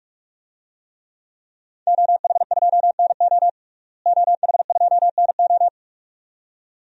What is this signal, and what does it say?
Morse code, heard as a keyed tone, about 46 wpm, 700 Hz. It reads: OH2NO OH2NO
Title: DXCC-100 Call Signs - 5 Characters - Repeated 2x - 45wpm